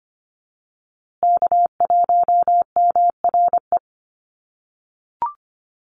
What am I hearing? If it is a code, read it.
Morse code, heard as a keyed tone, about 25 wpm, 700 Hz. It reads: K1MRE
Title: USA Call Signs Repeated 2x 25wpm